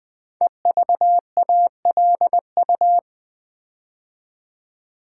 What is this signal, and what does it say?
Morse code, heard as a keyed tone, about 20 wpm, 700 Hz. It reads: EVALU
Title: Build A Word - 5 Letters At A Time - 10-Letter Words - 20wpm